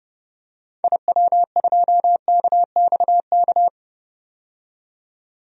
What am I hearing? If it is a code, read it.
Morse code, heard as a keyed tone, about 32 wpm, 700 Hz. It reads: IW2KXK
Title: DXCC-100 Call Signs - 6 Characters - 30wpm